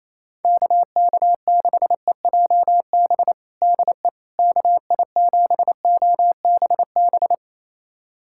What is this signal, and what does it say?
Morse code, heard as a keyed tone, about 27 wpm, 700 Hz. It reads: KK6EJB DE KI7OBB
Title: USA Callsign DE Callsign - 6 Characters - 28wpm